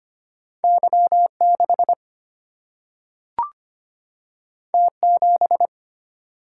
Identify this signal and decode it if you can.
Morse code, heard as a keyed tone, about 25 wpm, 700 Hz. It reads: Y6 T7
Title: Single-Letter Single-Number 25wpm